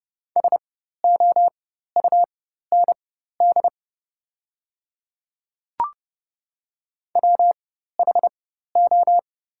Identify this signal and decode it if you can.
Morse code, heard as a keyed tone, about 30 wpm, spaced as for 14 wpm, 690 Hz. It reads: SOUND WHO
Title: 4x Character Spacing - Top 100 Words - 30wpm